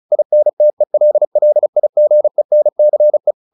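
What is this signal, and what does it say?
Morse code, heard as a keyed tone, about 34 wpm, 600 Hz. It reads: INTELLIGENCE
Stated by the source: Morse practice at 35 wpm from CW med Gnister, sending Ken Follett's novel Never in English.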